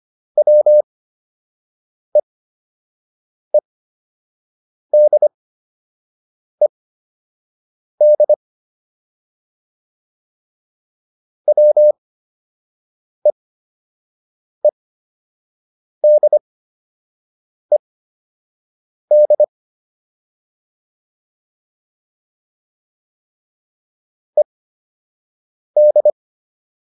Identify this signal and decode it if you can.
Morse code, heard as a keyed tone, about 25 wpm, spaced as for 6 wpm, 600 Hz. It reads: WEEDED WEEDED ED